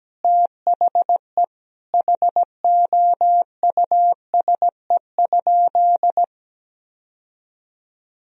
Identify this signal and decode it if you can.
Morse code, heard as a keyed tone, about 17 wpm, 700 Hz. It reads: THE HOUSE?
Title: Sentences from Top 1000 Words - Review Part 2 - 17wpm